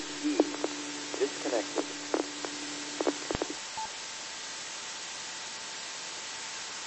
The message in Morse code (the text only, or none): none